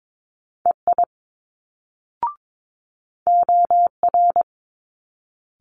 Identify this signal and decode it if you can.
Morse code, heard as a keyed tone, about 22 wpm, 700 Hz. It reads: EI OR
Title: Multi-letter Phonemes Quick Fire 22wpm